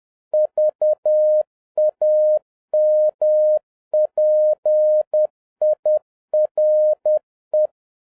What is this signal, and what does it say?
Morse code, heard as a keyed tone, about 10 wpm, 610 Hz. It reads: VAMPIRE